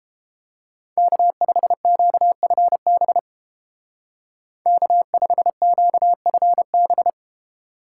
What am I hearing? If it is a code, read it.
Morse code, heard as a keyed tone, about 33 wpm, 700 Hz. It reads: K5QFB K5QFB